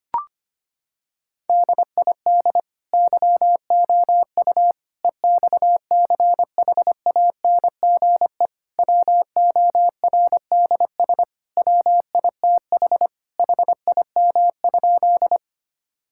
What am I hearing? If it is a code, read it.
Morse code, heard as a keyed tone, about 25 wpm, 700 Hz. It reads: DID YOU EXCHANGE WORDS WITH HIM?
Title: Sentences from Top 1000 Words - Review Part 6 - 25wpm